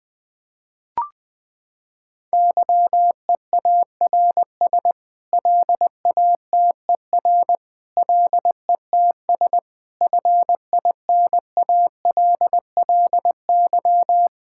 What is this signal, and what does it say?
Morse code, heard as a keyed tone, about 20 wpm, 700 Hz. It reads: YEARS LATER LETS FINALLY
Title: Sets of 8 Words - 20wpm